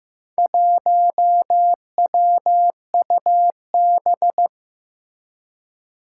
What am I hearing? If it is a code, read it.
Morse code, heard as a keyed tone, about 15 wpm, 700 Hz. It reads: 1WUB
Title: Four Challenging Letter-Number Combinations 15wpm